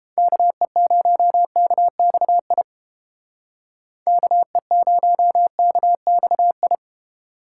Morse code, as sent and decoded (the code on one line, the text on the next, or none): -.- . ----- -.- -..- .. / -.- . ----- -.- -..- ..
KE0KXI KE0KXI